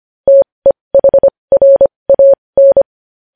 Morse code, heard as a keyed tone, about 25 wpm, spaced as for 20 wpm, 550 Hz. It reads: TEHRAN
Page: News Headlines in Morse Code at 20 WPM